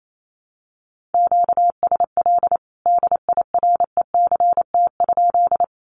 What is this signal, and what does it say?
Morse code, heard as a keyed tone, about 28 wpm, 700 Hz. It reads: QSL DIRECT?
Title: Common QSO Phrases 28wpm